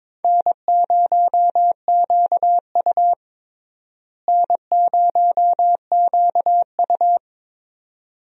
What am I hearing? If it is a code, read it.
Morse code, heard as a keyed tone, about 22 wpm, 700 Hz. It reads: N0QU N0QU
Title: USA Call Signs - 4 Characters - Repeated 2x - 22wpm